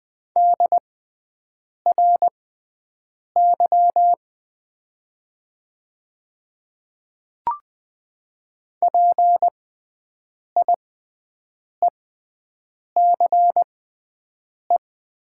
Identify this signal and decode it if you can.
Morse code, heard as a keyed tone, about 20 wpm, spaced as for 7 wpm, 700 Hz. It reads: DRY PIECE
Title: 6x Character Spacing - Top 500 Words - 20wpm